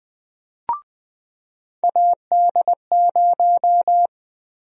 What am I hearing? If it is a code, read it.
Morse code, heard as a keyed tone, about 20 wpm, 700 Hz. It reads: AD0
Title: USA Call Sign Prefixes - 20wpm